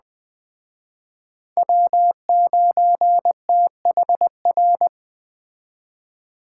Morse code, heard as a keyed tone, about 20 wpm, 700 Hz. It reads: W9THR
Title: Speed Racing - USA 1x3 Call Signs 17wpm